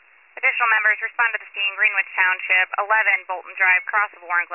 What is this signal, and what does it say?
Emergency Alert System (SAME) tones, heard as harsh buzzing bursts, with speech. It is heard from 1.0 to 3.9 s.